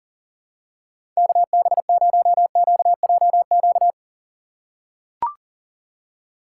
Morse code, heard as a keyed tone, about 40 wpm, 700 Hz. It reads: KB0QJQ
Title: USA Call Signs - 6 Characters - 4x Character Spacing - 40wpm